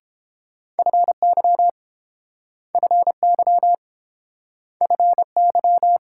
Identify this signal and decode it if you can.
Morse code, heard as a keyed tone, about 30 wpm, 700 Hz. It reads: FY FY FY